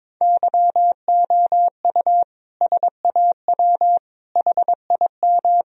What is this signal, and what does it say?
Morse code, heard as a keyed tone, about 22 wpm, 700 Hz. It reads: YOU SAW HIM
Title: Sentences from Top 1000 Words - Review Part 3 - 22wpm